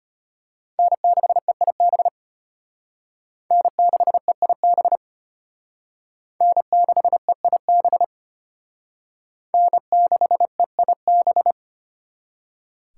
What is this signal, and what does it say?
Morse code, heard as a keyed tone, about 33 wpm, 700 Hz. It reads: N6EIB N6EIB N6EIB N6EIB